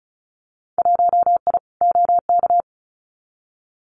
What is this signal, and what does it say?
Morse code, heard as a keyed tone, about 36 wpm, 700 Hz. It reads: 1I OK
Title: Winter Field Day Exchanges Encore 35wpm